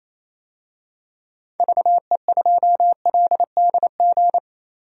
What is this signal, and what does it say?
Morse code, heard as a keyed tone, about 27 wpm, 700 Hz. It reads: VE2LDG
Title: DXCC-100 Call Signs - 6 Characters - 28wpm